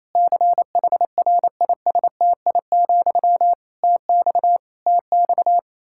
Morse code, heard as a keyed tone, about 28 wpm, 700 Hz. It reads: CHRISTI, TX TX
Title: K1USN SST QSO Practice 28wpm